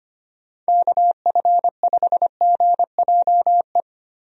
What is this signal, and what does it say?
Morse code, heard as a keyed tone, about 25 wpm, 700 Hz. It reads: KF5GJE
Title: USA Call Signs - 6 Characters - Repeated 3x - 25wpm